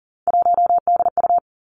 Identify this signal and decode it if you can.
Morse code, heard as a keyed tone, about 40 wpm, 700 Hz. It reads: 1DU
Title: Three Challenging Letter-Number Combinations 40wpm